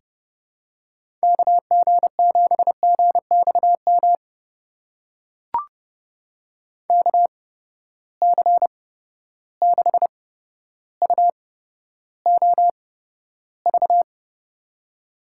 Morse code, heard as a keyed tone, about 30 wpm, spaced as for 13 wpm, 700 Hz. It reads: KG7GXM KC6UOV
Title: USA Call Signs - 6 Characters - 8x Character Spacing - 30wpm